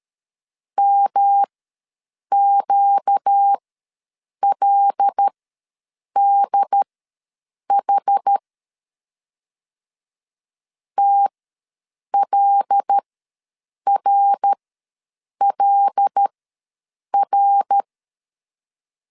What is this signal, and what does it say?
Morse code, heard as a keyed tone, about 13 wpm, spaced as for 7 wpm, 800 Hz. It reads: MQLDH TLRLR
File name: LEZIONI / 5 LBVC / 5 L ws 12wpm 2958